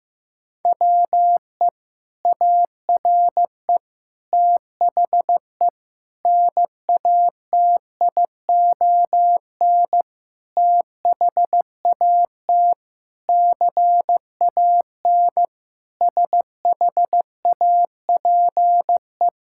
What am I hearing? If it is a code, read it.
Morse code, heard as a keyed tone, about 15 wpm, 700 Hz. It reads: WE ARE THE NATION THAT CAN SHAPE
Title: Sentences from Top 1000 Words - Review Part 6 - 15wpm